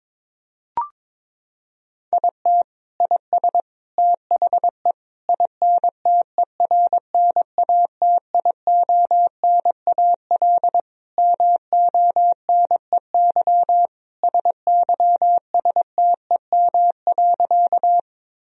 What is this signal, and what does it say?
Morse code, heard as a keyed tone, about 22 wpm, 700 Hz. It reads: IT IS THE INTERNATIONAL MONEY SYSTEM.
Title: Sentences from Top 1000 Words - Review Part 4 - 22wpm